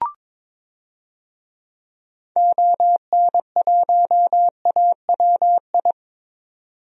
Morse code, heard as a keyed tone, about 22 wpm, 700 Hz. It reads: ON1AWI
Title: DXCC-100 Call Signs - 6 Characters - 22wpm